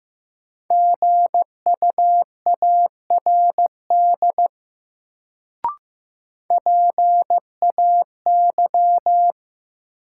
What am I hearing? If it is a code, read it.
Morse code, heard as a keyed tone, about 15 wpm, 700 Hz. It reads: GUARD PAY
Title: Sentences from Top 700 Words - New Word Drill - 15wpm